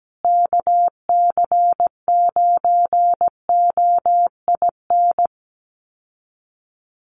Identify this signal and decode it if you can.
Morse code, heard as a keyed tone, about 17 wpm, 700 Hz. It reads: KC9OIN
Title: USA Call Signs Repeated 2x 17wpm